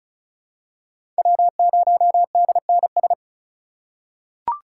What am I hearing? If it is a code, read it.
Morse code, heard as a keyed tone, about 34 wpm, 700 Hz. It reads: W0DNS
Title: USA Call Signs - 5 Characters - 4x Character Spacing - 35wpm